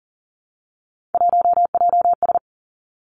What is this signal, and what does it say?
Morse code, heard as a keyed tone, about 40 wpm, 700 Hz. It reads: 1JS
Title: Three Challenging Letter-Number Combinations 40wpm